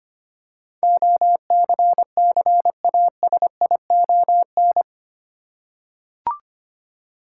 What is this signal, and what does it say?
Morse code, heard as a keyed tone, about 25 wpm, 700 Hz. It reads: OCCASION
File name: Eight Letter Words - 25wpm